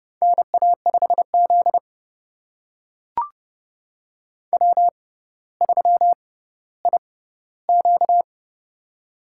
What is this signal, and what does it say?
Morse code, heard as a keyed tone, about 30 wpm, spaced as for 15 wpm, 700 Hz. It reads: NA5Z W3IQ
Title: USA Call Signs - 4 Characters - 6x Character Spacing - 30wpm